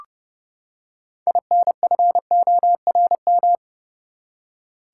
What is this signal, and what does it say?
Morse code, heard as a keyed tone, about 31 wpm, 700 Hz. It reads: INFORM
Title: Six Letter Words - 30wpm